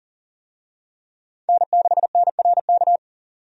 Morse code, heard as a keyed tone, about 37 wpm, 700 Hz. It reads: N6NRK
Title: USA Call Signs - 5 Characters - Repeated 2x - 40wpm